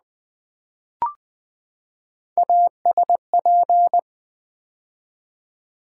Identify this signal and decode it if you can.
Morse code, heard as a keyed tone, about 20 wpm, 700 Hz. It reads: ASP
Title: Build A Word - 3 Letters At A Time - 6-Letter Words - 20wpm